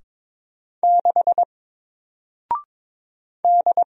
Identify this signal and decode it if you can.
Morse code, heard as a keyed tone, about 22 wpm, 700 Hz. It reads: 6 D
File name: Single-Letter-Number-22wpm